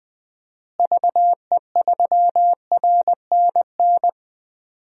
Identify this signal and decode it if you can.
Morse code, heard as a keyed tone, about 21 wpm, 700 Hz. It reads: VE3RNN
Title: DXCC-100 Call Signs - 6 Characters - Repeated 2x - 20wpm